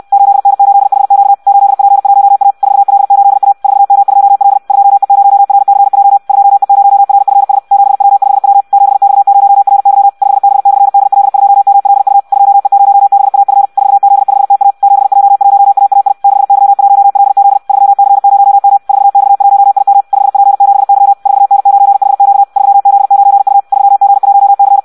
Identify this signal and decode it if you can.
Morse code, heard as a keyed tone, about 64 wpm, 800 Hz. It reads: 8N8HQ 8P1A 4L8A 4M2L 2E0RCJ 2E0RFS 7W5W 7X0RY 5B/DL2MLU 2E0BAF 4Z5TA 7J1AAI 6Y1LZ 3C0M 4X1IM 5P7Y 4N9HQ 3Z9U 3G1C
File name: Audio practica CW / qrq-tx-66 2